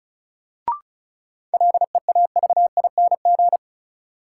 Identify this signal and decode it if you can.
Morse code, heard as a keyed tone, about 36 wpm, 700 Hz. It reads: LEAVING
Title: Sentences from Top 1000 Words - Review Part 4 - New Word Drill - 35wpm